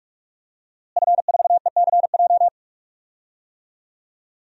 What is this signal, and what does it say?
Morse code, heard as a keyed tone, about 41 wpm, 700 Hz. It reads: F4ECJ
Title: DXCC-100 Call Signs - 5 Characters - Repeated 2x - 45wpm